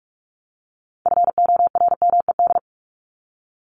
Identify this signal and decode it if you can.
Morse code, heard as a keyed tone, about 44 wpm, 700 Hz. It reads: FORMED
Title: ED Suffix - Top 100 Words with Base Words 45wpm